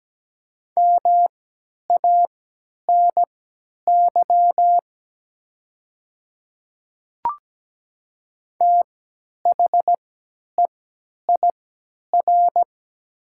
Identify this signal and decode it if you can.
Morse code, heard as a keyed tone, about 17 wpm, spaced as for 10 wpm, 700 Hz. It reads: MANY THEIR
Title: 3x Character Spacing - Top 100 Words - 17wpm